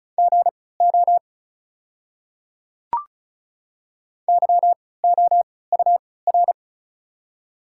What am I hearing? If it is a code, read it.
Morse code, heard as a keyed tone, about 34 wpm, spaced as for 20 wpm, 700 Hz. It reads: GO YOUR
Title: 3x Character Spacing - Top 100 Words - 35wpm